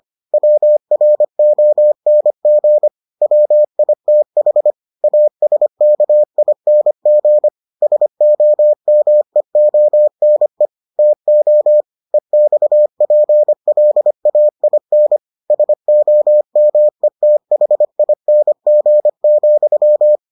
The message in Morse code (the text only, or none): WRONG WITH ASKING SOMEONE TO EXPLAIN SOMETHING,